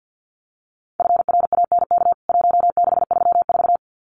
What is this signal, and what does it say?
Morse code, heard as a keyed tone, about 50 wpm, 700 Hz. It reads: FRANK 1634
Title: CWT Exchange 50wpm